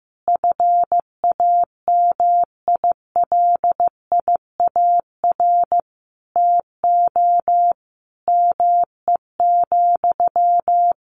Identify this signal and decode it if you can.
Morse code, heard as a keyed tone, about 14 wpm, 700 Hz. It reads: FAMILIAR TO ME,
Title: Sentences from Top 1000 Words 15wpm